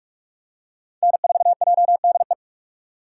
Morse code, heard as a keyed tone, about 47 wpm, 690 Hz. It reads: N4JDE
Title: USA Call Signs - 5 Characters - 6x Character Spacing - 45wpm